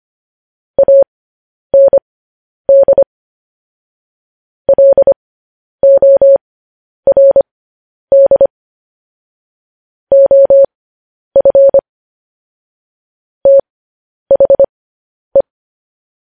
Morse code, heard as a keyed tone, about 25 wpm, spaced as for 10 wpm, 550 Hz. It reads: AND LORD OF THE